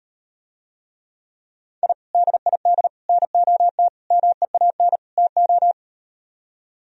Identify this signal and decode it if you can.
Morse code, heard as a keyed tone, about 39 wpm, 700 Hz. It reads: I DID NOT MEAN TO